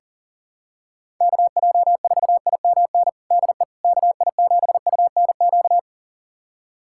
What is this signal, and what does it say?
Morse code, heard as a keyed tone, about 40 wpm, 700 Hz. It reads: KJ4IMN DE KI7UNQ
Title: USA Callsign DE Callsign - 6 Characters - 40wpm